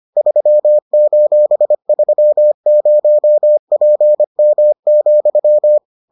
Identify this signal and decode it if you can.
Morse code, heard as a keyed tone, about 25 wpm, 600 Hz. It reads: 3:30PM,